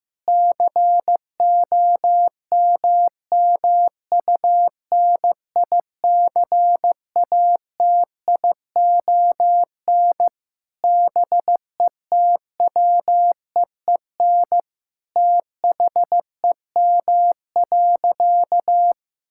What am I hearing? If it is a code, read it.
Morse code, heard as a keyed tone, about 15 wpm, 700 Hz. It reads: COMMUNICATION BETWEEN THEM.
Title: Sentences from Top 1000 Words - Review Part 6 - 15wpm